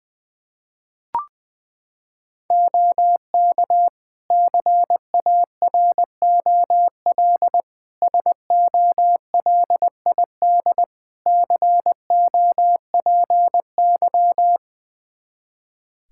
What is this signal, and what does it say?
Morse code, heard as a keyed tone, about 21 wpm, 700 Hz. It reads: OK CAROL SOLID COPY
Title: K1USN SST QSO Practice 20wpm